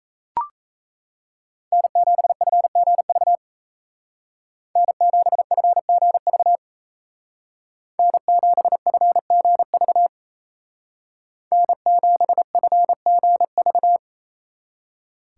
Morse code, heard as a keyed tone, about 33 wpm, 700 Hz. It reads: N7FGV N7FGV N7FGV N7FGV
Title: Speed Racing - USA Call Signs 28wpm